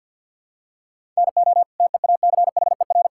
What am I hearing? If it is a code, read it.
Morse code, heard as a keyed tone, about 50 wpm, 700 Hz. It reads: NO TEACHER
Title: Sentences from Top 1000 Words - Review Part 6 - 50wpm